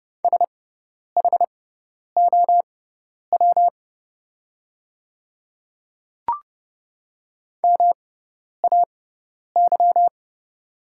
Morse code, heard as a keyed tone, about 30 wpm, spaced as for 10 wpm, 690 Hz. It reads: SHOW MAY
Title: 6x Character Spacing - Top 500 Words - 30wpm